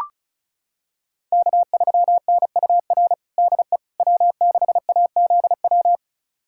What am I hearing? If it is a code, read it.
Morse code, heard as a keyed tone, about 36 wpm, 700 Hz. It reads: K3NUR DE W6AZW
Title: USA Callsign DE Callsign - 5 Characters - 35wpm